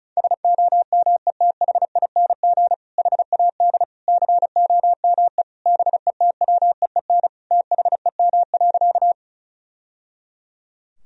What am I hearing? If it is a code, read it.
Morse code, heard as a keyed tone, about 36 wpm, 700 Hz. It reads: SOMETHING HAD COME BETWEEN THEM.
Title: Sentences from Top 1000 Words - Review Part 1 - 35wpm